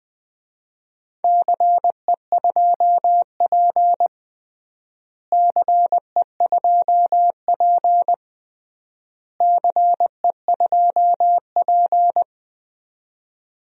Morse code, heard as a keyed tone, about 20 wpm, 700 Hz. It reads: CE2P CE2P CE2P
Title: DXCC-100 Call Signs - 4 Characters - Repeated 3x - 20wpm